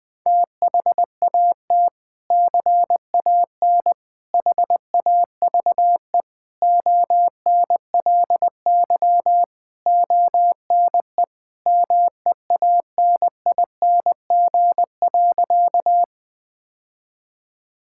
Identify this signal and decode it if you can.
Morse code, heard as a keyed tone, about 20 wpm, 700 Hz. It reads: THAT CAN HAVE ONLY ONE MEANING.
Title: Sentences from Top 1000 Words - Review Part 6 - 20wpm